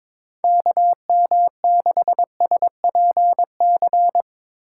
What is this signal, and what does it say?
Morse code, heard as a keyed tone, about 24 wpm, 700 Hz. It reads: KM6SPC
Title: Speed Racing - USA Call Signs 22wpm